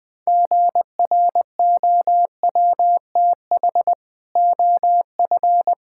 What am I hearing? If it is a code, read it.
Morse code, heard as a keyed tone, about 20 wpm, 700 Hz. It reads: GROWTH OF
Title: Sentences from Top 1000 Words - Review Part 5 - 20wpm